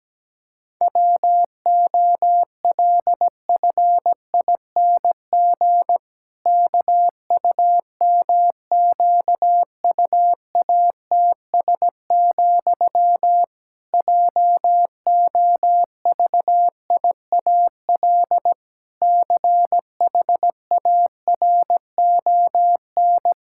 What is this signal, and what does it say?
Morse code, heard as a keyed tone, about 17 wpm, 700 Hz. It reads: WOLFING KUMQUATS, JOVIAL CHARON
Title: Pangram Challenge v2 - 17wpm